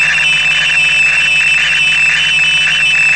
Morse code, heard as a keyed tone, about 17 wpm, 3000 Hz, with noise. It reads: OXZ